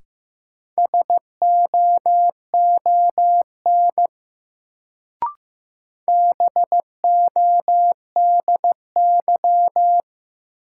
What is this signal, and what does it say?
Morse code, heard as a keyed tone, about 15 wpm, 700 Hz. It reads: SOON BODY
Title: Sentences from Top 400 Words - New Word Drill - 15wpm